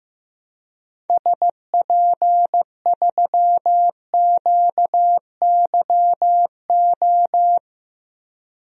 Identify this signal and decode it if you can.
Morse code, heard as a keyed tone, about 15 wpm, 700 Hz. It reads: SP3QYO
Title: DXCC-100 Call Signs - 6 Characters - Repeated 3x - 15wpm